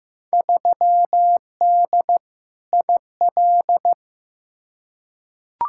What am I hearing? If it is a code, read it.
Morse code, heard as a keyed tone, about 15 wpm, 700 Hz. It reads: 3D IL
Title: Field Day Exchanges Encore - 15wpm